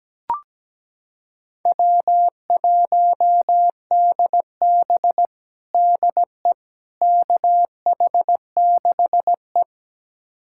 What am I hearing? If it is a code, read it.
Morse code, heard as a keyed tone, about 17 wpm, 700 Hz. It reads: W1DB DE KH6E
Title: USA Callsign DE Callsign - 4 Characters - 17wpm